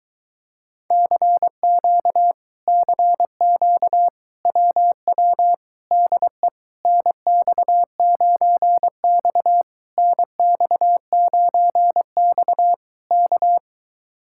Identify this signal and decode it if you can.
Morse code, heard as a keyed tone, about 23 wpm, 700 Hz. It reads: CQ CQ WW DE NX9X NX9X K